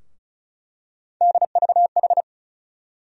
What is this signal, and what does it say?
Morse code, heard as a keyed tone, about 36 wpm, 700 Hz. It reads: DVH
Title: USA Call Sign Postfixes - 3 Letters 35wpm